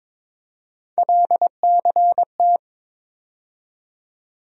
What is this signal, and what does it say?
Morse code, heard as a keyed tone, about 22 wpm, 700 Hz. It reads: LCT
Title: USA Call Sign Postfixes - 3 Letters 22wpm